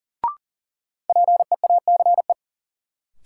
Morse code, heard as a keyed tone, about 40 wpm, 700 Hz. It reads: PEACE
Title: Sentences from Top 1500 Words - New Word Drill - 40wpm